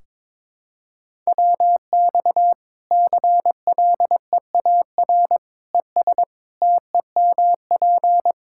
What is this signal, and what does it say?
Morse code, heard as a keyed tone, about 22 wpm, 700 Hz. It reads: WX CLEAR ES TEMP
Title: Practice QSO - Exchange 2 - Info - 22wpm